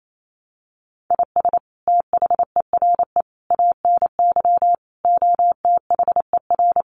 Morse code, heard as a keyed tone, about 28 wpm, 700 Hz. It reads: IS THERE ANY OTHER
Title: Sentences from Top 1100 Words 28wpm